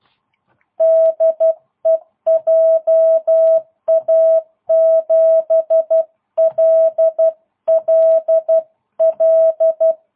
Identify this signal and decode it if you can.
Morse code, heard as a keyed tone, about 12 wpm, 660 Hz. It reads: DE JA7LLL